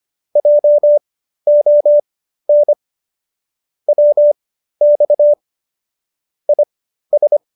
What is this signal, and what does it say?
Morse code, heard as a keyed tone, about 25 wpm, spaced as for 13 wpm, 590 Hz. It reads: JON WX IS